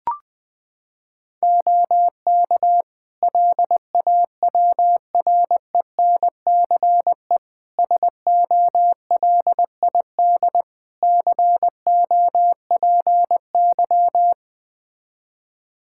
Morse code, heard as a keyed tone, about 21 wpm, 700 Hz. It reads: OK LAWRENCE SOLID COPY
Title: Practice QSO - Exchange 3 - More Info - 20wpm